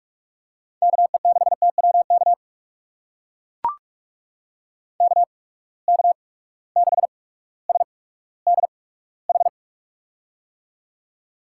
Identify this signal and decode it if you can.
Morse code, heard as a keyed tone, about 42 wpm, spaced as for 19 wpm, 700 Hz. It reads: KE6TWK KK6SDH
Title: USA Call Signs - 6 Characters - 8x Character Spacing - 45wpm